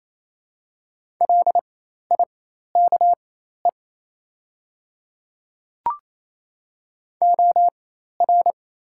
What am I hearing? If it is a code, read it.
Morse code, heard as a keyed tone, about 28 wpm, spaced as for 13 wpm, 700 Hz. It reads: LIKE OR
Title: 4x Character Spacing - Top 100 Words - 28wpm